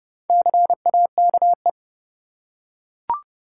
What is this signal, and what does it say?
Morse code, heard as a keyed tone, about 31 wpm, 700 Hz. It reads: CAKE
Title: Four Letter Words - 30wpm